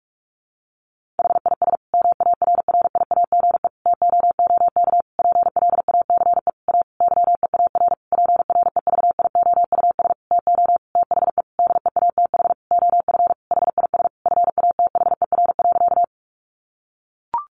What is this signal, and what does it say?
Morse code, heard as a keyed tone, about 44 wpm, 700 Hz. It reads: HIS MARRIAGE TOOK PLACE A YEAR PREVIOUS TO THE DEATH OF HIS FATHER.
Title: Sentences from Top 1500 Words 45wpm